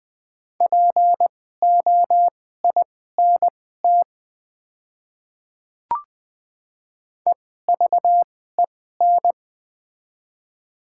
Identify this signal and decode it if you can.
Morse code, heard as a keyed tone, about 20 wpm, spaced as for 15 wpm, 700 Hz. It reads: POINT EVEN
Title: 2x Character Spacing - Top 200 Words - 20wpm